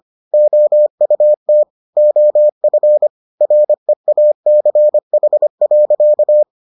Morse code, heard as a keyed tone, about 25 wpm, 600 Hz. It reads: OUT OF REACH.